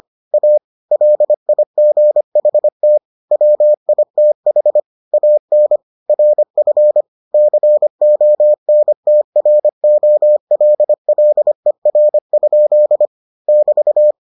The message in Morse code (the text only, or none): A LIGHT WITH AN RF CONTROLLER? <BT>